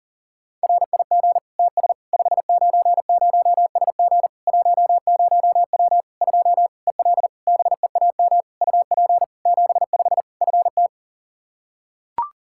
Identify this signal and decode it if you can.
Morse code, heard as a keyed tone, about 37 wpm, 700 Hz. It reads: RIG TS 590SG 10W 2 EL BEAM UP 75 FT
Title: Practice QSO - Exchange 2 - Info - 40wpm